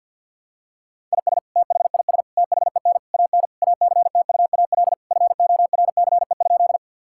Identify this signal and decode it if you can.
Morse code, heard as a keyed tone, about 50 wpm, 700 Hz. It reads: IS THIS THEN AN ACTUAL FORCE?
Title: Sentences from Top 1000 Words - Review Part 5 - 50wpm